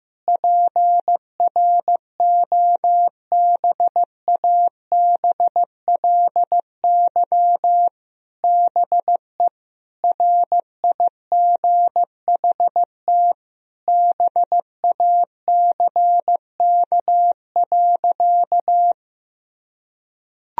Morse code, heard as a keyed tone, about 15 wpm, 700 Hz. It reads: PROBABLY BE RIGHT BACK.
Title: Sentences from Top 1000 Words - Review Part 4 - 15wpm